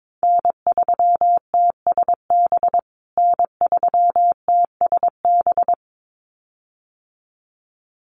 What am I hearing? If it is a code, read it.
Morse code, heard as a keyed tone, about 22 wpm, 700 Hz. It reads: N3TSB N3TSB
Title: USA Call Signs Repeated 2x 22wpm